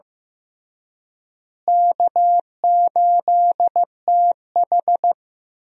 Morse code, heard as a keyed tone, about 15 wpm, 700 Hz. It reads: K8TH
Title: USA Call Signs - 4 Characters - Repeated 3x - 15wpm